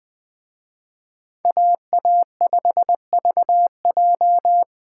Morse code, heard as a keyed tone, about 24 wpm, 700 Hz. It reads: AA5VJ
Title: Speed Racing - USA 2x2 Call Signs 20wpm